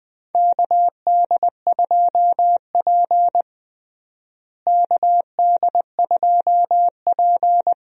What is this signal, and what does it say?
Morse code, heard as a keyed tone, about 20 wpm, 700 Hz. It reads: KD2P KD2P
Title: USA Call Signs - 4 Characters - Repeated 3x - 20wpm